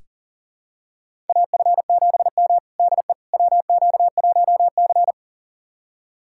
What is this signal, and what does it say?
Morse code, heard as a keyed tone, about 40 wpm, 700 Hz. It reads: AF7M DE WQ1C
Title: USA Callsign DE Callsign - 4 Characters - 40wpm